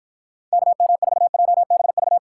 Morse code, heard as a keyed tone, about 53 wpm, 700 Hz. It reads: KM4JBV